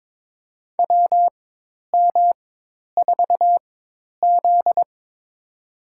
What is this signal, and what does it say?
Morse code, heard as a keyed tone, about 22 wpm, spaced as for 10 wpm, 700 Hz. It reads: WM4Z
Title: USA Call Signs - 4 Characters - 4x Character Spacing - 22wpm